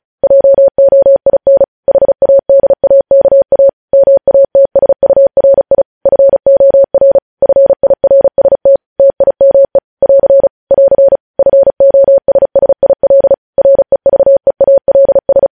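Morse code, heard as a keyed tone, about 36 wpm, 550 Hz. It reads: JOIN HADAKA MATSURI FOR FIRST TIME <AR> <AR> FOSSIL REVEALS
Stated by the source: Morse code news headlines practice sent at 35 wpm.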